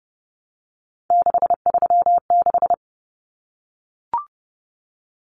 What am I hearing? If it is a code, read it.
Morse code, heard as a keyed tone, about 30 wpm, 700 Hz. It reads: 636
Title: Common Cut Numbers - 3-Digit 30wpm